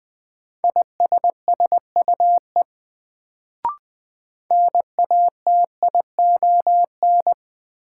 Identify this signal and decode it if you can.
Morse code, heard as a keyed tone, about 21 wpm, 700 Hz. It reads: ISSUE NATION
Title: Sentences from Top 900 Words - New Word Drill - 20wpm